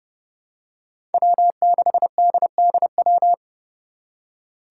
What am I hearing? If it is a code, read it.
Morse code, heard as a keyed tone, about 30 wpm, 700 Hz. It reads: W6DDW
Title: Five Challenging Letter-Number Combinations 30wpm